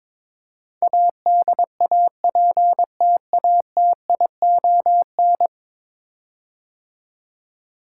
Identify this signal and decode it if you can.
Morse code, heard as a keyed tone, about 22 wpm, 700 Hz. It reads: ADAPTATION
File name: practice / 22wpm / 10 Letter Words - First 3 Letters Given - 22wpm